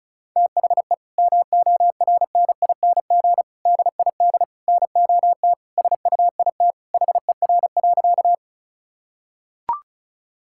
Morse code, heard as a keyed tone, about 34 wpm, 700 Hz. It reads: THE MORNING DID NOT SUIT HER.
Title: Sentences from Top 1600 Words 35wpm